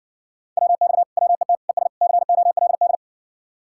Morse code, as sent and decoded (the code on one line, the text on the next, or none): .-- -..- / .-- . - / . ... / -.-. --- .-.. -..
WX WET ES COLD